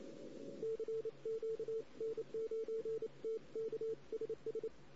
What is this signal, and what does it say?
Morse code, heard as a keyed tone, about 27 wpm, 450 Hz. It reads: CQ N9TK SS
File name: CW / 2008-ss-cw-k5tr-op-n5rz -16:00